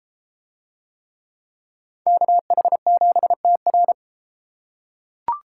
Morse code, heard as a keyed tone, about 30 wpm, 700 Hz. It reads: KH7TR